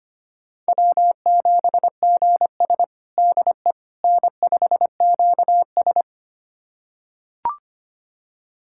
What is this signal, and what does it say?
Morse code, heard as a keyed tone, about 25 wpm, 700 Hz. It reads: W7GS DE N5QS
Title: USA Callsign DE Callsign - 4 Characters - 25wpm